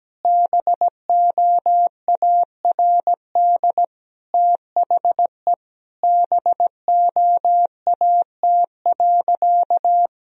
Morse code, heard as a keyed tone, about 17 wpm, 700 Hz. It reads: BOARD THE BOAT.
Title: Sentences from Top 1000 Words - Review Part 4 - 17wpm